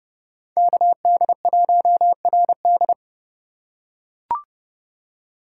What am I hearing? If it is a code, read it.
Morse code, heard as a keyed tone, about 30 wpm, 700 Hz. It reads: KD1RD